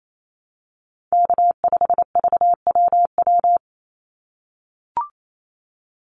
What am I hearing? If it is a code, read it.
Morse code, heard as a keyed tone, about 28 wpm, 700 Hz. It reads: K5VWW
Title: Call Signs - 28wpm